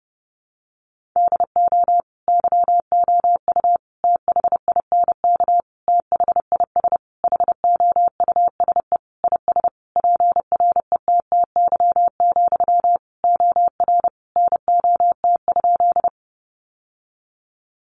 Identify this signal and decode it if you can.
Morse code, heard as a keyed tone, about 30 wpm, 700 Hz. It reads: DO YOU THINK THIS HOUSE IS PRETTY, OR NOT?
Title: Sentences from Top 1000 Words 30wpm